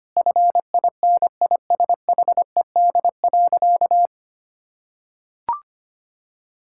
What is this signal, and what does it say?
Morse code, heard as a keyed tone, about 25 wpm, 700 Hz. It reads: FINISHED.
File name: practice / 25wpm / Sentences with 3-Word Phrases - 25wpm